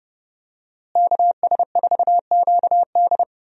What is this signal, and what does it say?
Morse code, heard as a keyed tone, about 30 wpm, 700 Hz. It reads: KS4QD
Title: USA Call Signs - 5 Characters - Repeated 2x - 30wpm